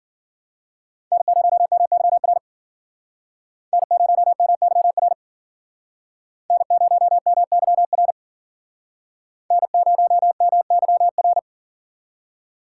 Morse code, heard as a keyed tone, about 52 wpm, 700 Hz. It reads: N0MYR N0MYR N0MYR N0MYR